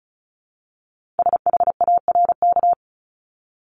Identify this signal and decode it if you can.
Morse code, heard as a keyed tone, about 36 wpm, 700 Hz. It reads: SHARK
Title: Multi-letter Phonemes 35wpm